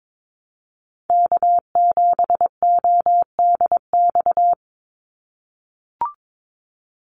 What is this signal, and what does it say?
Morse code, heard as a keyed tone, about 22 wpm, 700 Hz. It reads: K7ODX
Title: NA SOTA Activator Callsigns 22wpm